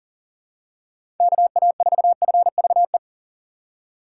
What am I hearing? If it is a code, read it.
Morse code, heard as a keyed tone, about 37 wpm, 700 Hz. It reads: KA4FVE
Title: USA Call Signs - 6 Characters - 3x Character Spacing - 40wpm